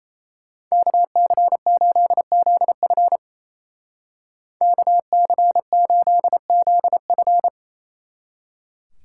Morse code, heard as a keyed tone, about 35 wpm, 700 Hz. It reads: KC8ZF KC8ZF
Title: Speed Racing - USA Call Signs 28wpm